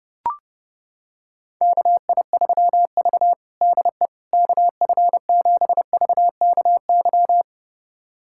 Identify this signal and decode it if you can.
Morse code, heard as a keyed tone, about 30 wpm, 700 Hz. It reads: KI3V DE KF7VKY
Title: USA Callsign DE Callsign 30wpm